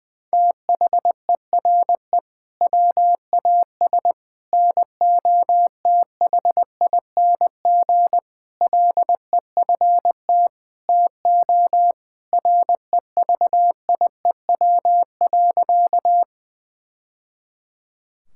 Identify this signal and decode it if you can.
Morse code, heard as a keyed tone, about 20 wpm, 700 Hz. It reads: THERE WAS NOTHING LEFT TO REVIEW.
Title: Sentences from Top 1000 Words - Review Part 6 - 20wpm